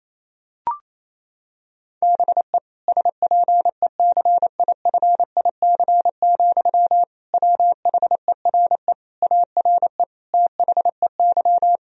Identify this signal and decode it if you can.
Morse code, heard as a keyed tone, about 28 wpm, 700 Hz. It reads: BE SPECIFIC, WHERE ARE THEY
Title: Sentences from Top 1000 Words - Review Part 3 - 28wpm